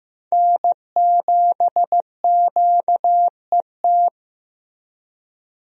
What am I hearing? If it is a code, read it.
Morse code, heard as a keyed tone, about 15 wpm, 700 Hz. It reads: N7QET
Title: USA Call Signs - 5 Characters - Repeated 2x - 15wpm